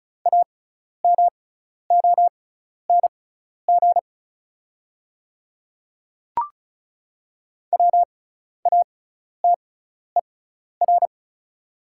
Morse code, heard as a keyed tone, about 36 wpm, spaced as for 12 wpm, 700 Hz. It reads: AMONG WATER
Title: 6x Character Spacing - Top 500 Words - 35wpm